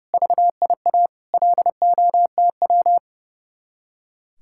Morse code, heard as a keyed tone, about 37 wpm, 700 Hz. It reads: VIA LOTW
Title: Speed-Racing Common QSO Phrases - 30wpm